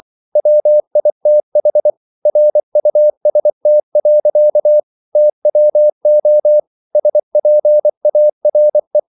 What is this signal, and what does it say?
Morse code, heard as a keyed tone, about 24 wpm, 600 Hz. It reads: WITH RUST. TWO SPARE